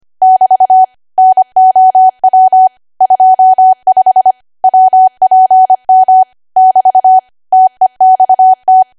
Morse code, heard as a keyed tone, about 25 wpm, 750 Hz. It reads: <BT> NOW 25 WPM <BT> TEXT
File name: Archive / 25 WPM / 25WPM